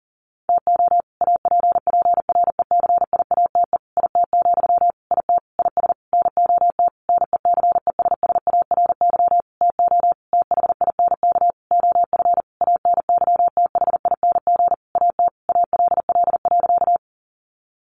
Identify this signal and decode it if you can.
Morse code, heard as a keyed tone, about 40 wpm, 700 Hz. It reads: TO APPRECIATE IT, IT IS NOT NECESSARY TO THINK OF ANYTHING AT ALL.
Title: Sentences from Top 1300 Words 40wpm